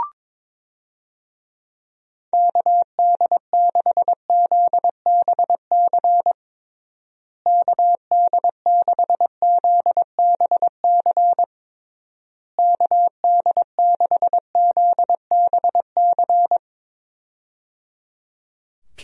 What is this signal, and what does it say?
Morse code, heard as a keyed tone, about 22 wpm, 700 Hz. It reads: KD6ZBC KD6ZBC KD6ZBC
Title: USA Call Signs - 6 Characters - Repeated 3x - 22wpm